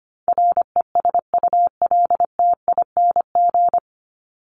Morse code, heard as a keyed tone, about 25 wpm, 700 Hz. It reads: RESULTING
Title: ING Suffix - Top 100 Words with Base Words 25wpm